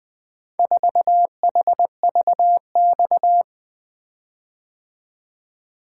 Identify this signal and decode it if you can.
Morse code, heard as a keyed tone, about 20 wpm, 700 Hz. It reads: 4HVX
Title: Four Letter-Number Combinations - 20wpm